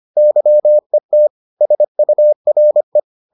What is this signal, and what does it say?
Morse code, heard as a keyed tone, about 25 wpm, 600 Hz. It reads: YET SURE